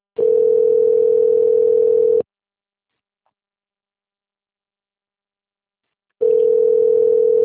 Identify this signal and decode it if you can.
Ringback tone (North America), 440 + 480 Hz.
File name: rg-1600-unknown- - - .1944